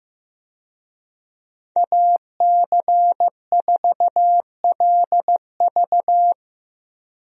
Morse code, heard as a keyed tone, about 15 wpm, 700 Hz. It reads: AC4LV
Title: USA Call Signs - 5 Characters - Repeated 2x - 15wpm